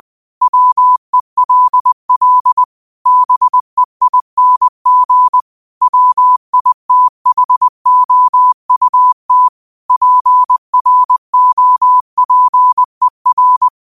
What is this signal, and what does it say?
Morse code, heard as a keyed tone, about 20 wpm, 1000 Hz. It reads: WELL BEING WITHOUT PROPER
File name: qotdfiles / 20wpm